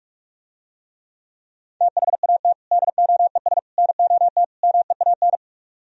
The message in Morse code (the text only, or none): THAT DOES NOT MEAN